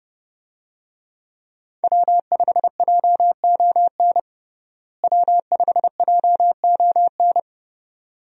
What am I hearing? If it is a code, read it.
Morse code, heard as a keyed tone, about 32 wpm, 700 Hz. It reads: W5JON W5JON
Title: USA Call Signs - 5 Characters - Repeated 2x - 30wpm